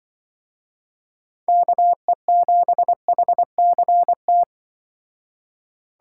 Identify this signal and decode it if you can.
Morse code, heard as a keyed tone, about 24 wpm, 700 Hz. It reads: KE7HCT
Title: Speed Racing - USA 2x3 Call Signs 20wpm